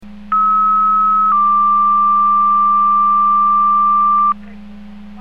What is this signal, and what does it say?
Two-tone page, heard as a one-second tone, then a three-second tone, about 1290 then 1180 Hz. It is heard from 0.3 to 4.3 s.